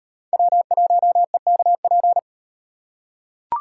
Morse code, heard as a keyed tone, about 38 wpm, 700 Hz. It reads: W1EKP